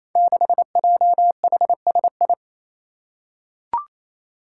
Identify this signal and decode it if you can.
Morse code, heard as a keyed tone, about 29 wpm, 700 Hz. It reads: 6JHSI
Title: Five Challenging Letter-Number Combinations 28wpm